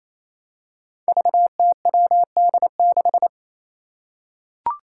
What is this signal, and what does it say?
Morse code, heard as a keyed tone, about 27 wpm, 700 Hz. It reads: VTWD6
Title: Five Challenging Letter-Number Combinations 28wpm